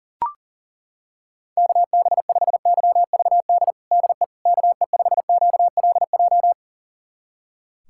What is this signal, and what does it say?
Morse code, heard as a keyed tone, about 37 wpm, 700 Hz. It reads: KB5YVD DE KE5QLJ
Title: USA Callsign DE Callsign - 6 Characters - 40wpm